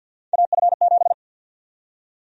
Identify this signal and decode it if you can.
Morse code, heard as a keyed tone, about 52 wpm, 700 Hz. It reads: AF7
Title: USA Call Sign Prefixes - 50wpm